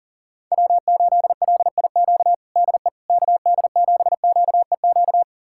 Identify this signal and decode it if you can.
Morse code, heard as a keyed tone, about 40 wpm, 700 Hz. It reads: W8LIQ DE KD7QEQ
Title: USA Callsign DE Callsign 40wpm